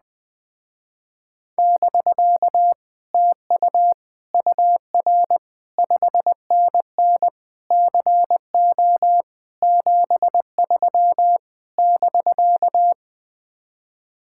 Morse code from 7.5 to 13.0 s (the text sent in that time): CO 73 <BK>